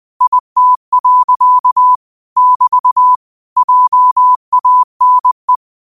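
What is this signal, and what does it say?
Morse code, heard as a keyed tone, about 20 wpm, 1000 Hz. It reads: IT. <BT> JANE
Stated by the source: 20 WPM Morse code quotes, QOTD at 20 WPM.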